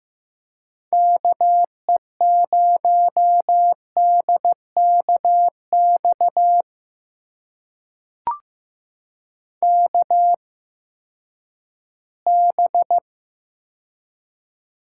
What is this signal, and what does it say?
Morse code, heard as a keyed tone, about 15 wpm, spaced as for 10 wpm, 700 Hz. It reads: KE0DKX KB
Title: USA Call Signs - 6 Characters - 8x Character Spacing - 15wpm